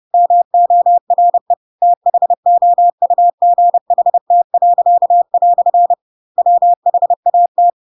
Morse code, heard as a keyed tone, about 30 wpm, 700 Hz. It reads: MORE THOUGHT." WHAT
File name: Pinocchio / Pinocchio ch3 30 30 ik1hge